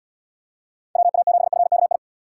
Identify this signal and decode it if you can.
Morse code, heard as a keyed tone, about 79 wpm, 700 Hz. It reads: WA7FKI